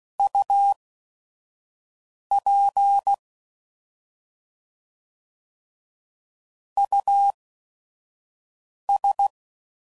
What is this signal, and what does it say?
Morse code, heard as a keyed tone, about 16 wpm, spaced as for 5 wpm, 800 Hz. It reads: UP US
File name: Prefixes / By First Letter / Us